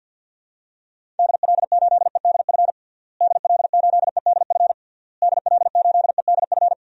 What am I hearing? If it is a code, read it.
Morse code, heard as a keyed tone, about 49 wpm, 700 Hz. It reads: DL8EDF DL8EDF DL8EDF
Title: DXCC-100 Call Signs - 6 Characters - Repeated 3x - 50wpm